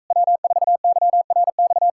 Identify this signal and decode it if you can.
Morse code, heard as a keyed tone, about 42 wpm, 700 Hz. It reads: W4YRX